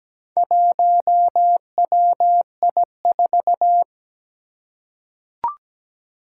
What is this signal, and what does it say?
Morse code, heard as a keyed tone, about 17 wpm, 700 Hz. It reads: 1WI4
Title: Four Challenging Letter-Number Combinations 17wpm